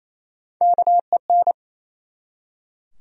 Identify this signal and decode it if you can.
Morse code, heard as a keyed tone, about 28 wpm, 700 Hz. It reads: KEN
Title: USA Common Names - Max 4 Letters - 28wpm